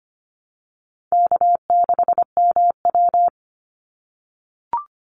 Morse code, heard as a keyed tone, about 25 wpm, 700 Hz. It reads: K6MW
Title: NA SOTA Chaser Callsigns 25wpm